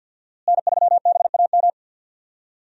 Morse code, heard as a keyed tone, about 51 wpm, 700 Hz. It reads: N3BAM
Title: USA Call Signs - 5 Characters - Repeated 2x - 50wpm